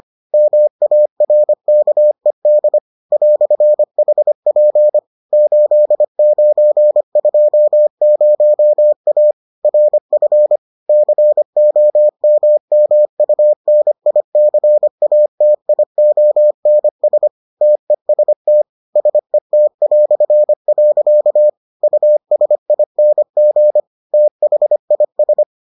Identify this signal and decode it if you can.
Morse code, heard as a keyed tone, about 25 wpm, 600 Hz. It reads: MARKED "HP 8920A RF COMMUNICATIONS TEST SET". USING THIS